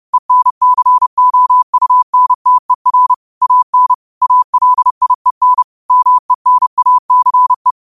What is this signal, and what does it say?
Morse code, heard as a keyed tone, about 30 wpm, 1000 Hz. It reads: ENCOUNTER AN ALIEN MENACE
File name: qotdfiles / 30wpm